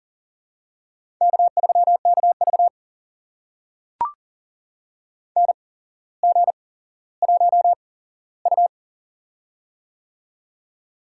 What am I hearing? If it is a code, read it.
Morse code, heard as a keyed tone, about 40 wpm, spaced as for 17 wpm, 700 Hz. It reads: K3KV NG1U
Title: USA Call Signs - 4 Characters - 8x Character Spacing - 40wpm